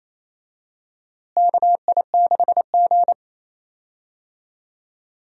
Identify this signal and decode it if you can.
Morse code, heard as a keyed tone, about 28 wpm, 700 Hz. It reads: KI6G
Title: USA Call Signs - 4 Characters - Repeated 3x - 28wpm